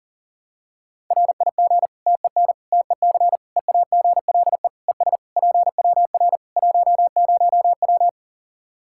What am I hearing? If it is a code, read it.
Morse code, heard as a keyed tone, about 40 wpm, 700 Hz. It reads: RIG TEN TEC EAGLE ES PWR 10W
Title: K1USN SST QSO Practice 40wpm